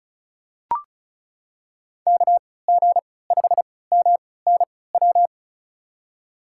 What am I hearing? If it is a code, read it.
Morse code, heard as a keyed tone, about 36 wpm, spaced as for 20 wpm, 700 Hz. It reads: KG5MNW
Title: USA Call Signs - 3x Character Spacing - 35wpm